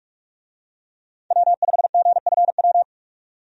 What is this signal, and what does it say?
Morse code, heard as a keyed tone, about 44 wpm, 700 Hz. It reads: W5GFW
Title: USA Call Signs - 5 Characters - 45wpm